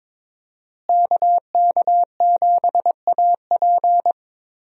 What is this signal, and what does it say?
Morse code, heard as a keyed tone, about 22 wpm, 700 Hz. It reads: KK7AP
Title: USA Call Signs - 5 Characters - Repeated 3x - 22wpm